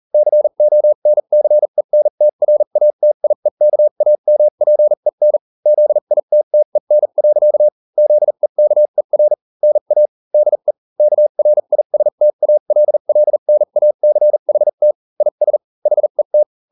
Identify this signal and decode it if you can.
Morse code, heard as a keyed tone, about 40 wpm, spaced as for 35 wpm, 600 Hz. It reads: CONCENTRATIEKAMPEN ZITTEN. ZEKER NA DE KRISTALLNACHT IS HET